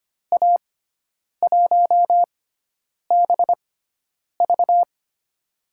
Morse code, heard as a keyed tone, about 25 wpm, spaced as for 9 wpm, 700 Hz. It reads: A1BV
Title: USA Call Signs - 5 Characters - 6x Character Spacing - 25wpm